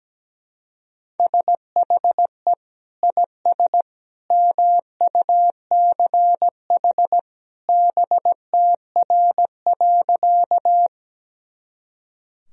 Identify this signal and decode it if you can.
Morse code, heard as a keyed tone, about 17 wpm, 700 Hz. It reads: SHE IS MUCH BTR.
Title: Sentences from Top 400 Words and Abbreviations 17wpm